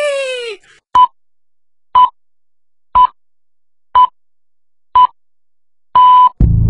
Radio time signal (time pips). It is heard from 1.0 to 6.3 s.